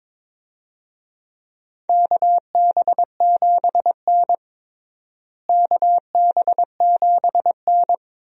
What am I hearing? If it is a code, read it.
Morse code, heard as a keyed tone, about 22 wpm, 700 Hz. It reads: KB7N KB7N
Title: USA Call Signs - 4 Characters - Repeated 3x - 22wpm